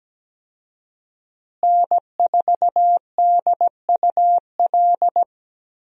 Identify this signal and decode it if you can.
Morse code, heard as a keyed tone, about 17 wpm, 700 Hz. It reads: N4DUL
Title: USA Call Signs - 5 Characters - Repeated 3x - 17wpm